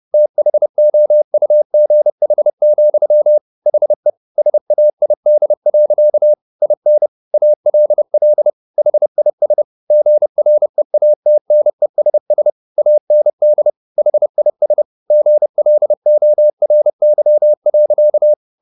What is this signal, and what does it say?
Morse code, heard as a keyed tone, about 30 wpm, 600 Hz. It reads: THOUGH, HE SAID. IN ALL HIS GREATNESS AND HIS GLORY.
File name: the old man and the sea / the old man and the sea 1 30wpm